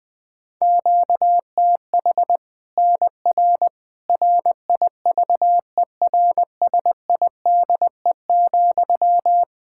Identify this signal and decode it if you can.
Morse code, heard as a keyed tone, about 21 wpm, 700 Hz. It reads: QTH NR RIVERSIDE,
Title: Practice QSO - Exchange 1 - Beginning - 20wpm